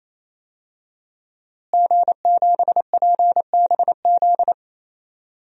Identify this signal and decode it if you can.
Morse code, heard as a keyed tone, about 28 wpm, 700 Hz. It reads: G7PBZ
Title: DXCC-100 Call Signs - 5 Characters - Repeated 3x - 28wpm